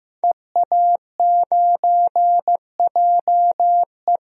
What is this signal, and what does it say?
Morse code, heard as a keyed tone, about 15 wpm, 700 Hz. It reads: EA9JE
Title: DXCC-100 Call Signs - 5 Characters - Repeated 2x - 15wpm